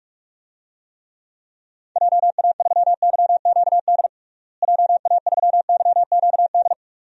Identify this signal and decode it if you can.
Morse code, heard as a keyed tone, about 46 wpm, 700 Hz. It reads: JA3YQD JA3YQD
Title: DXCC-100 Call Signs - 6 Characters - Repeated 3x - 45wpm